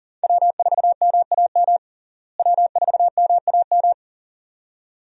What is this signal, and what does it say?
Morse code, heard as a keyed tone, about 37 wpm, 700 Hz. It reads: W4MAM W4MAM
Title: USA Call Signs - 5 Characters - Repeated 2x - 40wpm